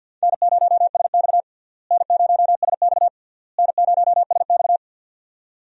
Morse code, heard as a keyed tone, about 46 wpm, 700 Hz. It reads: N0SX N0SX N0SX
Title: USA Call Signs - 4 Characters - Repeated 3x - 50wpm